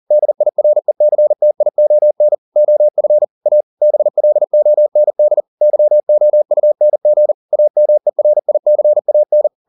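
Morse code, heard as a keyed tone, about 40 wpm, 600 Hz. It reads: DIRECTION OF A BLOND YOUNG AMERICAN
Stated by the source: Morse code at 40 wpm, practice text, CW med Gnister.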